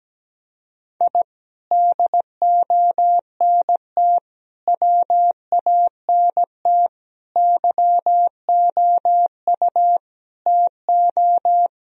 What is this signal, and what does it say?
Morse code, heard as a keyed tone, about 17 wpm, 700 Hz. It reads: I DONT WANT YOU TO
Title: Sentences from Top 1000 Words - Review Part 3 - 17wpm